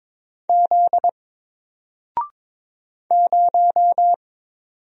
Morse code, heard as a keyed tone, about 22 wpm, 700 Hz. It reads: Z 0
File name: Single-Letter-Number-22wpm